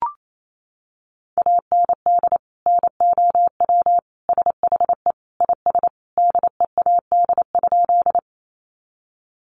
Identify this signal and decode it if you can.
Morse code, heard as a keyed tone, about 28 wpm, 700 Hz. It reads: AND NOW SHE IS DEAD?
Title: Sentences from Top 1500 Words 28wpm